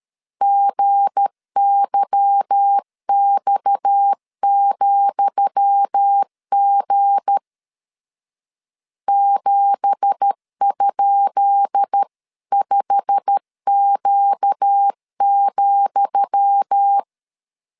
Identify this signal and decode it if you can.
Morse code, heard as a keyed tone, about 13 wpm, 800 Hz. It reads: GYX,G 7?5Q,
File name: LEZIONI / 10 SIGNS / 10D COMA 12wpm 2544